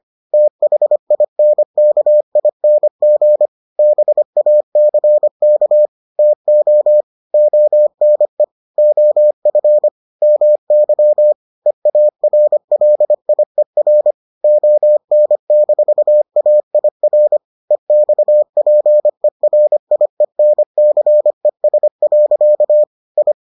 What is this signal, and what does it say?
Morse code, heard as a keyed tone, about 25 wpm, 600 Hz. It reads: THINKING BACK TO ONE OF MY EARLIER ON-AIR EXPERIENCES. I